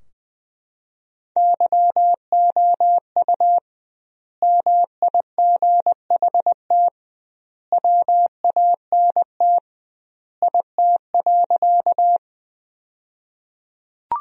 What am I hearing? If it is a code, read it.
Morse code, heard as a keyed tone, about 20 wpm, 700 Hz. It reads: YOU MIGHT WANT IT.
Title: 2x Word-Spacing - Sentences from Top 300 Words - 20wpm